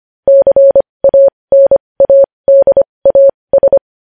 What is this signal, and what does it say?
Morse code, heard as a keyed tone, about 25 wpm, spaced as for 20 wpm, 550 Hz. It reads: CANADAS
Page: news headlines in morse code at 20 wpm